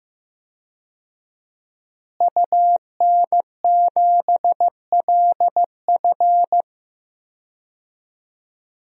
Morse code, heard as a keyed tone, about 15 wpm, 700 Hz. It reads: UN7LF
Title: DXCC-100 Call Signs - 5 Characters - Repeated 3x - 15wpm